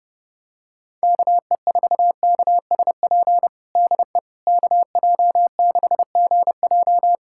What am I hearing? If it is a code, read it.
Morse code, heard as a keyed tone, about 32 wpm, 700 Hz. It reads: KE4KSP DE KJ6GJ
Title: USA Callsign DE Callsign 30wpm